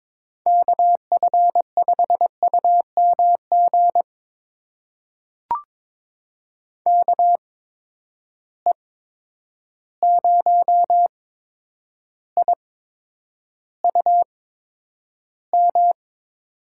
Morse code, heard as a keyed tone, about 22 wpm, spaced as for 9 wpm, 700 Hz. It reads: KF5UMG KE0IUM
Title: USA Call Signs - 6 Characters - 8x Character Spacing - 22wpm